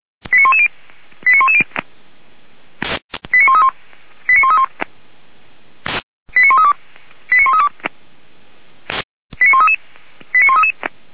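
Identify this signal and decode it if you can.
Five-tone selective call (ZVEI), heard as quick runs of five tones, tones of about 70 ms. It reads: 89110 89110 89131 89131 89132 89132 89133 89133
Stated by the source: ZVEI five-tone selective call received on SDR.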